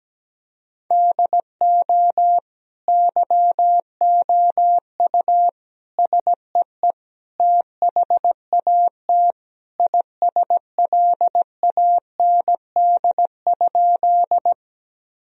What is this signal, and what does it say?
Morse code, heard as a keyed tone, about 17 wpm, 700 Hz. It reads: DO YOU SEE THAT ISLAND?
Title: Sentences from Top 1000 Words - Review Part 3 - 17wpm